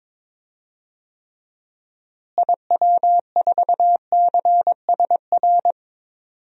Morse code, heard as a keyed tone, about 22 wpm, 700 Hz. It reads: IW4CSR
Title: DXCC-100 Call Signs - 6 Characters - Repeated 3x - 22wpm